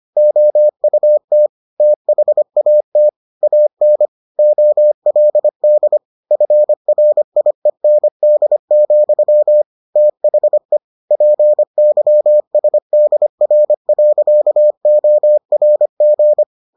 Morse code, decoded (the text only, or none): OUT THAT AN OLD FRIEND, THE PYSDR.ORG